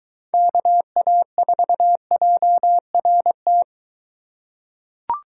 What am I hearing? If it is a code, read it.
Morse code, heard as a keyed tone, about 22 wpm, 700 Hz. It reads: KA4JRT